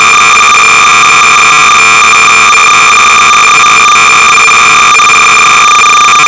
Cassette data (Kansas City type).
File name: BASIC-Sorcerer-300baud